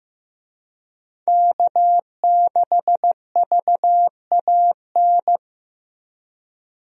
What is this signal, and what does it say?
Morse code, heard as a keyed tone, about 15 wpm, 700 Hz. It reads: K6VAN
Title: USA Call Signs - 5 Characters - 15wpm